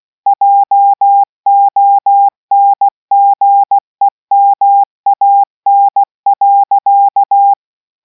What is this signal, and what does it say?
Morse code, heard as a keyed tone, about 16 wpm, 800 Hz. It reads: JONGEMAN.